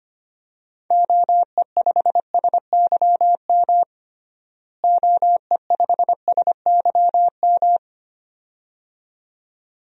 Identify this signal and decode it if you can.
Morse code, heard as a keyed tone, about 25 wpm, 700 Hz. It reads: OE5SYM OE5SYM
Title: DXCC-100 Call Signs - 6 Characters - Repeated 2x - 25wpm